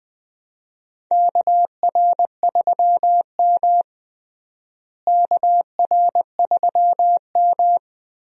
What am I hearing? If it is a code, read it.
Morse code, heard as a keyed tone, about 20 wpm, 700 Hz. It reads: KR3M KR3M
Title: USA Call Signs - 4 Characters - Repeated 3x - 20wpm